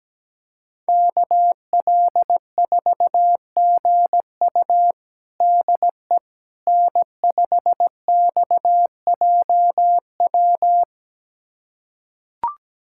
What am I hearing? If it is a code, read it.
Morse code, heard as a keyed tone, about 17 wpm, 700 Hz. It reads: KL4GU DE N5XJW
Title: USA Callsign DE Callsign - 5 Characters - 17wpm